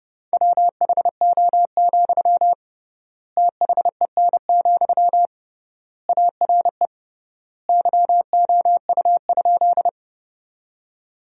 Morse code, heard as a keyed tone, about 30 wpm, 700 Hz. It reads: WHO, THEN, ARE YOU?
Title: 3x Word-Spacing - Sentences from Top 100 Words - 30wpm